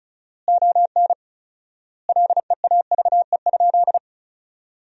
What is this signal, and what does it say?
Morse code, heard as a keyed tone, about 34 wpm, 700 Hz. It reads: ON LEAVE?
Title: 4x Word-Spacing - Sentences from Top 400 Words - 35wpm